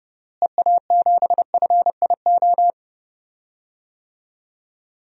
Morse code, heard as a keyed tone, about 30 wpm, 700 Hz. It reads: EA7FIO
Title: DXCC-100 Call Signs - 6 Characters - Repeated 2x - 30wpm